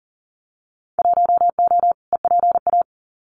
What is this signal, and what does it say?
Morse code, heard as a keyed tone, about 38 wpm, 700 Hz. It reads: 1O EPA
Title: Winter Field Day Exchanges 40wpm